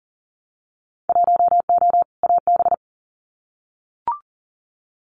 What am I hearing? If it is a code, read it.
Morse code, heard as a keyed tone, about 39 wpm, 700 Hz. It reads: 1O AB
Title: Winter Field Day Exchanges 40wpm